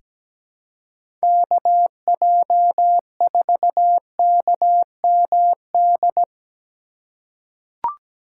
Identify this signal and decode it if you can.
Morse code, heard as a keyed tone, about 17 wpm, 700 Hz. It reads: KJ4KMD
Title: USA Call Signs - 6 Characters - 3x Character Spacing - 17wpm